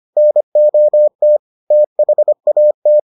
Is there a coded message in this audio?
Yes: Morse code from 0.2 to 3.0 s, heard as a keyed tone, about 25 wpm, 600 Hz. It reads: NOT THAT